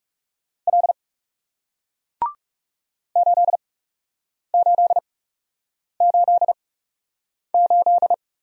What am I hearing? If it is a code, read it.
Morse code, heard as a keyed tone, about 37 wpm, spaced as for 8 wpm, 700 Hz. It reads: L 8888
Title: Speed-Racing Single Letter-Number 30wpm